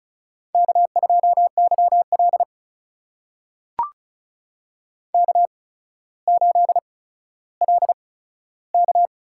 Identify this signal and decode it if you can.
Morse code, heard as a keyed tone, about 36 wpm, spaced as for 15 wpm, 700 Hz. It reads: K2YL K8LK
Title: USA Call Signs - 4 Characters - 8x Character Spacing - 35wpm